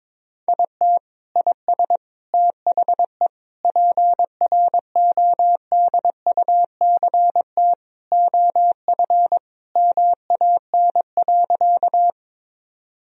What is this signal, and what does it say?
Morse code, heard as a keyed tone, about 22 wpm, 700 Hz. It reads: IT IS THE PRODUCT OF MAN.
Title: Sentences from Top 1000 Words - Review Part 2 - 22wpm